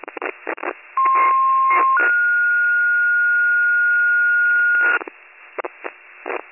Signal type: two-tone page

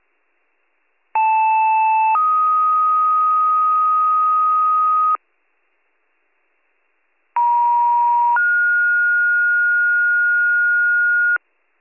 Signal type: two-tone page

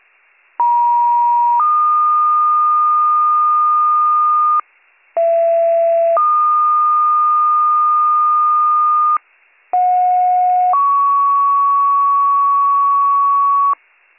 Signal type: two-tone page

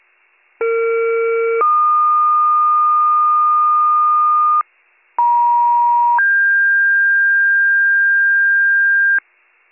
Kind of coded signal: two-tone page

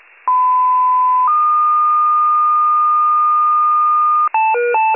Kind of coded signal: two-tone page

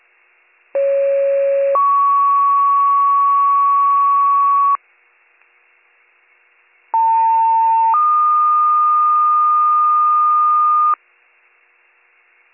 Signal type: two-tone page

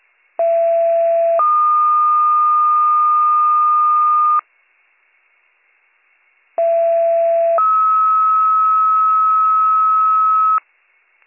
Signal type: two-tone page

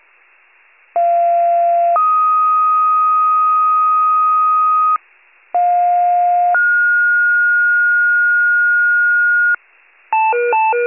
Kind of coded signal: two-tone page

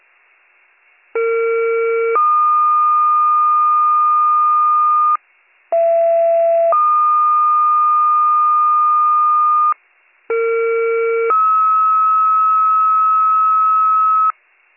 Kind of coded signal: two-tone page

